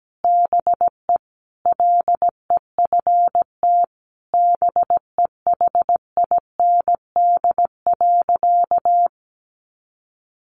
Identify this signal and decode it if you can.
Morse code, heard as a keyed tone, about 17 wpm, 700 Hz. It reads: BE LEFT BEHIND.
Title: A Sound Of Thunder 17wpm